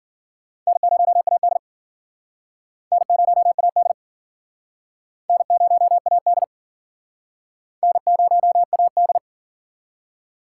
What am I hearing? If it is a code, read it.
Morse code, heard as a keyed tone, about 50 wpm, 700 Hz. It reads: N0AD N0AD N0AD N0AD